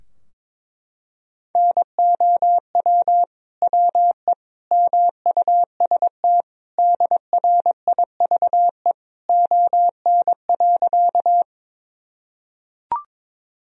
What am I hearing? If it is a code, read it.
Morse code, heard as a keyed tone, about 22 wpm, 700 Hz. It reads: NOW WE MUST DRIVE ON.
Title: Sentences from Top 1000 Words - Review Part 2 - 22wpm